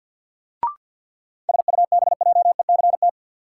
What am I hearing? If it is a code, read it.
Morse code, heard as a keyed tone, about 52 wpm, 700 Hz. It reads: SUBJECT
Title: Sentences from Top 800 Words - New Word Drill - 50wpm